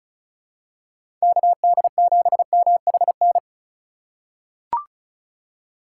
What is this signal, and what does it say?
Morse code, heard as a keyed tone, about 34 wpm, 700 Hz. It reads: KD7MHN
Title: USA Call Signs - 6 Characters - 3x Character Spacing - 35wpm